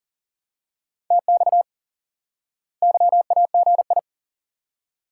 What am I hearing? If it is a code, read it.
Morse code, heard as a keyed tone, about 40 wpm, 700 Hz. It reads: TX YAGI
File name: IQE Course 3 - Lesson 89 - YAGI - Two Elements - 40wpm